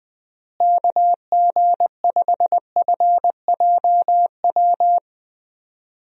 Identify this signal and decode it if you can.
Morse code, heard as a keyed tone, about 20 wpm, 700 Hz. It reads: KG5FJW
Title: USA Call Signs - 6 Characters - Repeated 2x - 20wpm